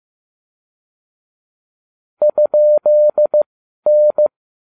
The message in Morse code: ?N

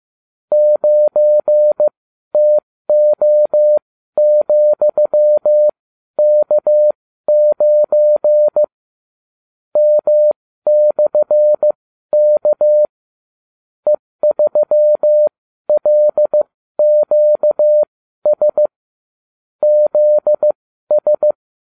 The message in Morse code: 9TO,K9 M/K E3LQS ZS

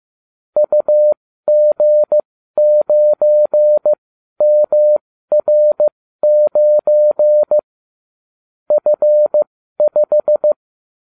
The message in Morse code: UG9MR9 F5

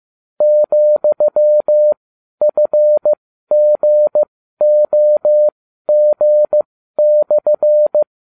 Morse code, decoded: ,FGOG/

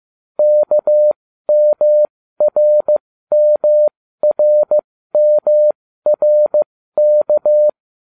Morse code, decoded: KMRMRMRK

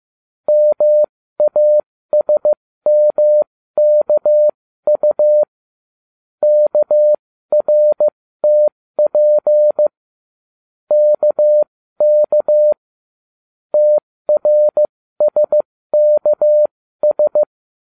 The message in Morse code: MASMKU KRTP KK TRSKS

